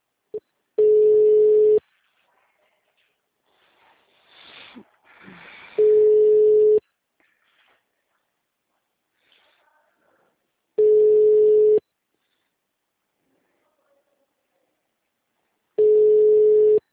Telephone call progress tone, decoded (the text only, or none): ringback tone